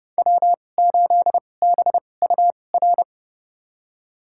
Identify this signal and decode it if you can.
Morse code, heard as a keyed tone, about 30 wpm, spaced as for 22 wpm, 700 Hz. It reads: W8BUR
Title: USA Call Signs - 5 Characters - 2x Character Spacing - 30wpm